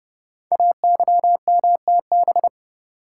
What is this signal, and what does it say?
Morse code, heard as a keyed tone, about 30 wpm, 700 Hz. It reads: AYMTB
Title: Code Groups - 5 Letters - 30wpm